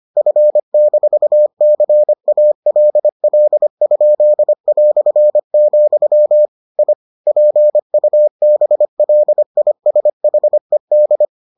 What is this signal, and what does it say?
Morse code, heard as a keyed tone, about 25 wpm, 600 Hz. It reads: F-CALL?", I PUBLISHED